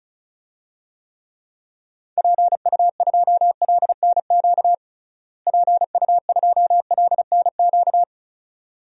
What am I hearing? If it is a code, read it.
Morse code, heard as a keyed tone, about 36 wpm, 700 Hz. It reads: PU2LNQ PU2LNQ
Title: DXCC-100 Call Signs - 6 Characters - Repeated 2x - 35wpm